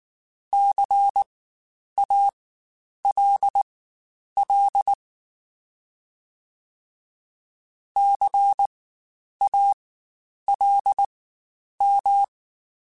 Morse code, heard as a keyed tone, about 19 wpm, spaced as for 9 wpm, 800 Hz. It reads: CALL CALM
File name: Spaced Alphas / Spaced Alphas mp3s / Cs sp @18